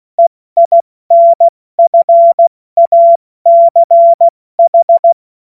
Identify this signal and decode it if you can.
Morse code, heard as a keyed tone, about 16 wpm, 680 Hz. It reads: EINFACH
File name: Morsekurs / Witze / Witz 30. August 2022 70 F80